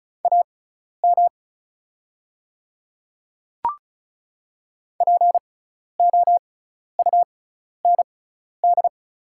Morse code, heard as a keyed tone, about 34 wpm, spaced as for 12 wpm, 690 Hz. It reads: AM POUND
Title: 6x Character Spacing - Top 500 Words - 35wpm